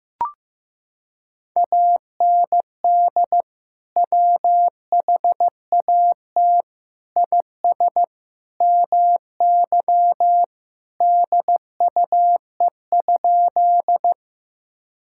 Morse code, heard as a keyed tone, about 15 wpm, 700 Hz. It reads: AND WHAT IS MY DUE?
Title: Sentences from Top 1000 Words - Review Part 6 - 15wpm